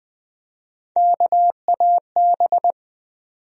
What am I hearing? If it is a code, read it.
Morse code, heard as a keyed tone, about 20 wpm, 700 Hz. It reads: KAB
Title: USA Call Sign Postfixes - 3 Letters 20wpm